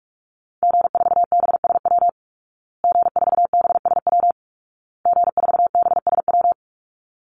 Morse code, heard as a keyed tone, about 44 wpm, 700 Hz. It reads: G4BSW G4BSW G4BSW
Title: Call Signs Repeated 2x - 45wpm